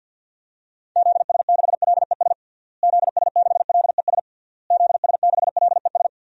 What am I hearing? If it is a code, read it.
Morse code, heard as a keyed tone, about 52 wpm, 700 Hz. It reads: ZS6LES ZS6LES ZS6LES
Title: DXCC-100 Call Signs - 6 Characters - Repeated 3x - 50wpm